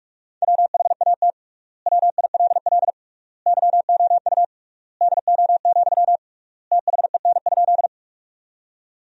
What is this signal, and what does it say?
Morse code, heard as a keyed tone, about 46 wpm, 690 Hz. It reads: WHAT WILL YOU DO, THEN?
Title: 3x Word-Spacing - Sentences from Top 100 Words - 45wpm